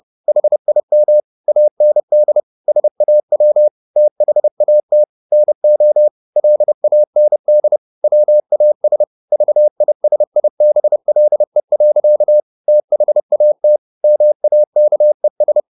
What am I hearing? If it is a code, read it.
Morse code, heard as a keyed tone, about 30 wpm, 600 Hz. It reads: HIM AND SAW THAT NO LAND WAS VISIBLE. THAT MAKES
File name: the old man and the sea / the old man and the sea 1 30wpm